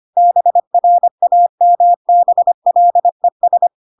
Morse code, heard as a keyed tone, about 25 wpm, 700 Hz. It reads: BRAMBLES
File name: Pinocchio / Pinocchio ch4 25 25 ik1hge